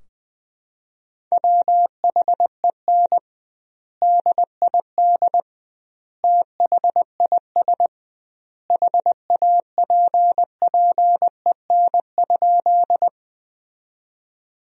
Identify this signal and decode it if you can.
Morse code, heard as a keyed tone, about 21 wpm, 700 Hz. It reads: WHEN DID THIS HAPPEN?
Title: 2x Word-Spacing - Sentences from Top 400 Words - 20wpm